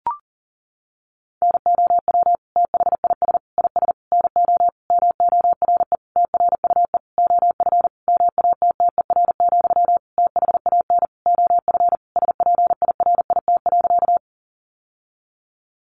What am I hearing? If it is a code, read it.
Morse code, heard as a keyed tone, about 40 wpm, 700 Hz. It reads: NOW THIS IS NO MORE TRUE OF MATTER, THAN OF SPIRIT.
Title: Sentences from Top 1400 Words 40wpm